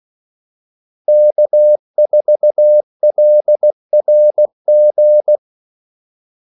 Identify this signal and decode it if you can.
Morse code, heard as a keyed tone, about 16 wpm, 600 Hz. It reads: K4LRG
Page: Kaldesignaler 16 wpm